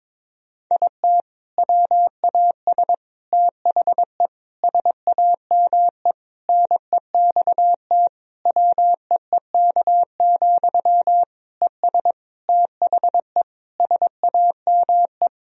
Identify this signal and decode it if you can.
Morse code, heard as a keyed tone, about 22 wpm, 700 Hz. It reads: IT WAS THE SAME NEXT WEEK, ES THE SAME
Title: Sentences from Top 500 Words and Abbreviations 22wpm